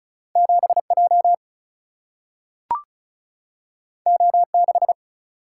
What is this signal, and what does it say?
Morse code, heard as a keyed tone, about 36 wpm, 700 Hz. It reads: 7J O6
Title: Single-Letter Single-Number 35wpm